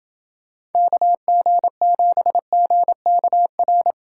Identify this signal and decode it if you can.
Morse code, heard as a keyed tone, about 29 wpm, 700 Hz. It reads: KG7GKR